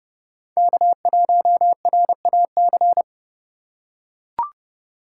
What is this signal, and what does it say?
Morse code, heard as a keyed tone, about 29 wpm, 700 Hz. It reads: K1RAC
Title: USA Call Signs - 5 Characters - 4x Character Spacing - 30wpm